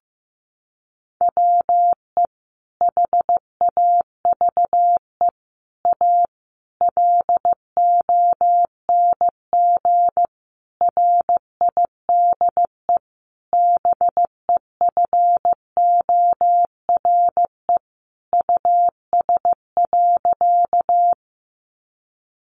Morse code, heard as a keyed tone, about 15 wpm, 700 Hz. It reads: WE HAVE A LONG RIDE BEFORE US.
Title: Sentences from Top 1200 Words 15wpm